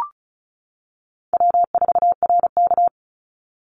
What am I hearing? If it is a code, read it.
Morse code, heard as a keyed tone, about 34 wpm, 700 Hz. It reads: W4RK
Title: NA SOTA Chaser Callsigns 35wpm